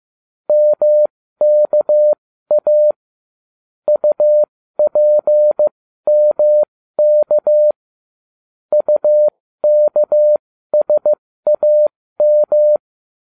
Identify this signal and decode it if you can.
Morse code, heard as a keyed tone, about 15 wpm, spaced as for 13 wpm, 600 Hz. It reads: MKA UPMK UKSAM